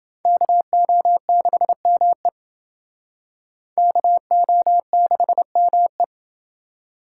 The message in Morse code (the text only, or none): KO6ME KO6ME